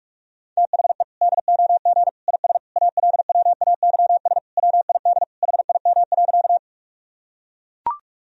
Morse code, heard as a keyed tone, about 41 wpm, 700 Hz. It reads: THE DOG IS ALWAYS WID HIM.
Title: Sentences from Top 400 Words and Abbreviations 45wpm